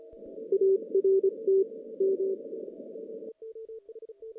Morse code, heard as a keyed tone, about 26 wpm, 390 Hz. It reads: ART M
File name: 2016 JAN CW NAQP / 2016 Jan naqp k5ot - 0-10